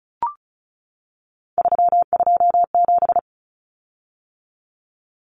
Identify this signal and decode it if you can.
Morse code, heard as a keyed tone, about 34 wpm, 700 Hz. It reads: 327
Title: Common Cut Numbers - 3-Digit 35wpm